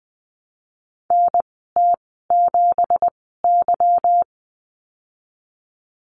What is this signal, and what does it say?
Morse code, heard as a keyed tone, about 20 wpm, spaced as for 15 wpm, 700 Hz. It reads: NT7Y
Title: 245 Lesson 39g - Letter z - Call Signs 20wpm